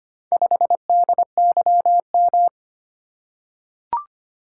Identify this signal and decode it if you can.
Morse code, heard as a keyed tone, about 25 wpm, 700 Hz. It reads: 5DYM
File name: practice / 25wpm / Four Letter-Number Combinations - 25wpm